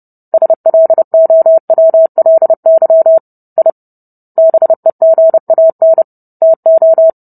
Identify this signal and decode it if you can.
Morse code, heard as a keyed tone, about 30 wpm, 650 Hz. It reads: SLOWLY I BEGAN TO